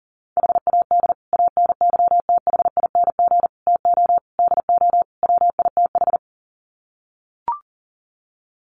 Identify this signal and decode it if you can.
Morse code, heard as a keyed tone, about 40 wpm, 700 Hz. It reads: HAD ANYTHING TO DO WITH
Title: Sets of 5 Words Encore 40wpm